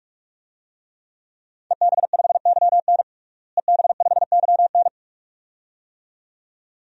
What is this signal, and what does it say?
Morse code, heard as a keyed tone, about 42 wpm, 700 Hz. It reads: EB5YN EB5YN
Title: DXCC-100 Call Signs - 5 Characters - Repeated 2x - 45wpm